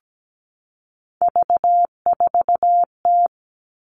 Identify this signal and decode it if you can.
Morse code, heard as a keyed tone, about 17 wpm, 700 Hz. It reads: V4T
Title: Three Challenging Letter-Number Combinations 17wpm